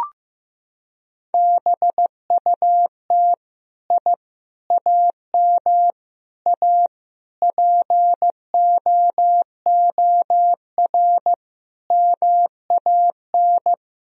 Morse code, heard as a keyed tone, about 15 wpm, 700 Hz. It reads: BUT I AM A POOR MAN
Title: Sentences from Top 1000 Words - Review Part 3 - 15wpm